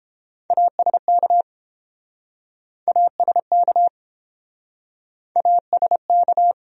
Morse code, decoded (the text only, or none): ASK ASK ASK